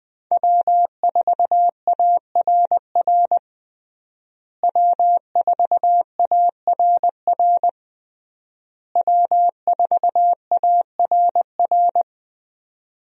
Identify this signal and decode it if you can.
Morse code, heard as a keyed tone, about 20 wpm, 700 Hz. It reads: W4ARR W4ARR W4ARR
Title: USA Call Signs - 5 Characters - Repeated 3x - 20wpm